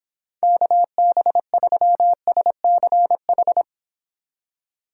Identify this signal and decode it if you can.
Morse code, heard as a keyed tone, about 25 wpm, 700 Hz. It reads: KB3SCH